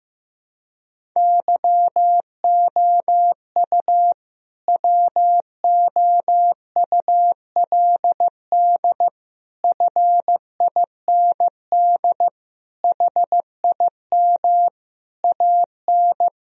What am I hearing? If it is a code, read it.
Morse code, heard as a keyed tone, about 15 wpm, 700 Hz. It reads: YOU WOULD FIND HIM AN
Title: Sentences from Top 1000 Words - Review Part 1 - 15wpm